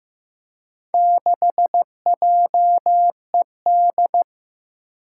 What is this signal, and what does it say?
Morse code, heard as a keyed tone, about 15 wpm, 700 Hz. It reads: 6JED
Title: Four Challenging Letter-Number Combinations 15wpm